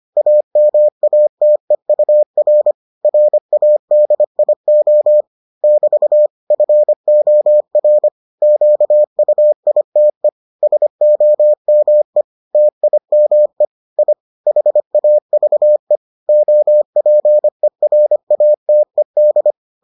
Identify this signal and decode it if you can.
Morse code, heard as a keyed tone, about 25 wpm, 600 Hz. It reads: AMATEUR RADIO <BT> FOR QUITE SOME TIME I HAVE OPERATED